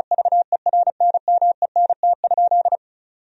Morse code, heard as a keyed tone, about 36 wpm, 700 Hz. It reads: VERNMENT?
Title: Sentences from Top 1000 Words - Review Part 4 - 35wpm